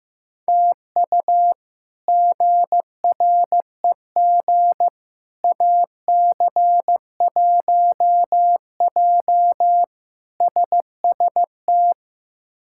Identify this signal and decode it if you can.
Morse code, heard as a keyed tone, about 15 wpm, 700 Hz. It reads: TU GREG AC1J SST
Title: K1USN SST QSO Practice - 15wpm